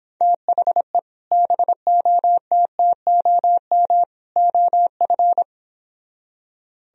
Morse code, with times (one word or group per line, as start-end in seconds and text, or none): 0.2-1.0 THE
1.3-4.0 BOTTOM
4.4-5.4 OF